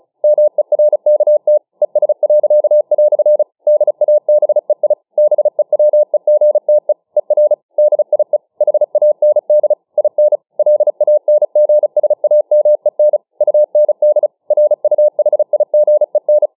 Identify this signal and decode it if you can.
Morse code, heard as a keyed tone, about 34 wpm, 600 Hz. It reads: MERKT ES." DABEI BEWEGTE ER DIE HAND IN LANGSAMEN UND RUHIGEN